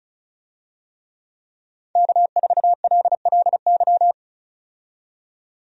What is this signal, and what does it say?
Morse code, heard as a keyed tone, about 36 wpm, 700 Hz. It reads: K4LLY
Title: USA Call Signs - 5 Characters - 35wpm